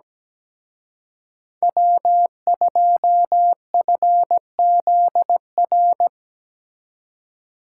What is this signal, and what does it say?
Morse code, heard as a keyed tone, about 19 wpm, 700 Hz. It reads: W2FZR
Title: Speed Racing - USA 1x3 Call Signs 17wpm